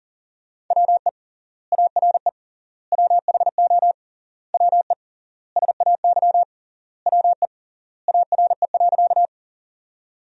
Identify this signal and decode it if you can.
Morse code, heard as a keyed tone, about 40 wpm, 700 Hz. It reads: WE ARE WHO WE SAY WE ARE.
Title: 3x Word-Spacing - Sentences from Top 200 Words - 40wpm